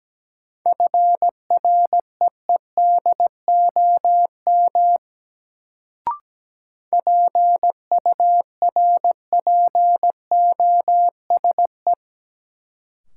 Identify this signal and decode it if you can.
Morse code, heard as a keyed tone, about 17 wpm, 700 Hz. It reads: FREEDOM PURPOSE
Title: Sentences from Top 800 Words - New Word Drill - 17wpm